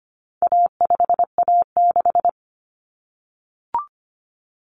Morse code, heard as a keyed tone, about 25 wpm, 700 Hz. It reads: A5A6
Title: Common Cut Numbers - 4-Digit 25wpm